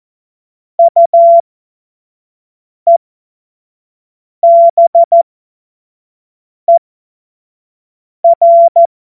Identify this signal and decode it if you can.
Morse code, heard as a keyed tone, about 14 wpm, spaced as for 5 wpm, 680 Hz. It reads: UEBER
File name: Morsekurs / Witze / Witz 07. Februar 2023 25 F70